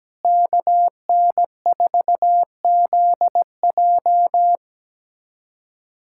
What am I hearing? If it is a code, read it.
Morse code, heard as a keyed tone, about 17 wpm, 700 Hz. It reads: KN4ZJ
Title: USA Callsign DE Callsign - 5 Characters - 17wpm